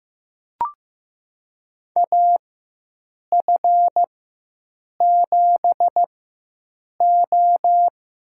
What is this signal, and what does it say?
Morse code, heard as a keyed tone, about 15 wpm, spaced as for 7 wpm, 700 Hz. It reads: AF7O
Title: USA Call Signs - 4x Character Spacing - 15wpm